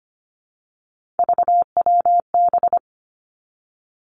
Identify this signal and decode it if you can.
Morse code, heard as a keyed tone, about 25 wpm, 700 Hz. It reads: VWB
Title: Three Challenging Letter-Number Combinations 25wpm